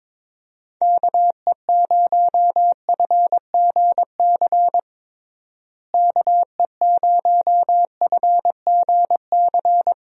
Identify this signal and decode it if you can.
Morse code, heard as a keyed tone, about 21 wpm, 700 Hz. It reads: KE0FGC KE0FGC
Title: USA Call Signs - 6 Characters - Repeated 3x - 22wpm